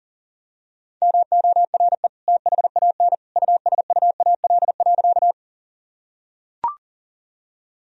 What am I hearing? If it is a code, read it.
Morse code, heard as a keyed tone, about 40 wpm, 700 Hz. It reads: MORE THAN USUAL.
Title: Sentences from Top 1000 Words - Review Part 2 - 40wpm